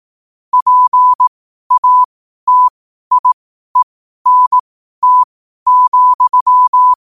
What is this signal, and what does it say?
Morse code, heard as a keyed tone, about 18 wpm, spaced as for 12 wpm, 1000 Hz. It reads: PATIENT,
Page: New quotes every day in morse code at 12 Words per minute.